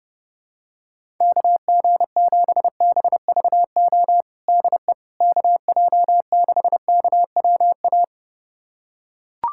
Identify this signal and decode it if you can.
Morse code, heard as a keyed tone, about 30 wpm, 700 Hz. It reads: KG7BVO DE KJ6KWA
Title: USA Callsign DE Callsign - 6 Characters - 30wpm